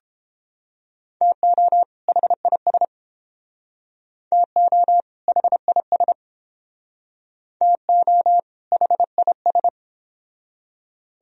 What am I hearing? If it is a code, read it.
Morse code, heard as a keyed tone, about 30 wpm, 700 Hz. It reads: TO HIS TO HIS TO HIS